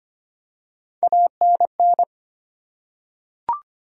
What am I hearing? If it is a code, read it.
Morse code, heard as a keyed tone, about 25 wpm, 700 Hz. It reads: ANN
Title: USA Common Names - Max 4 Letters - 25wpm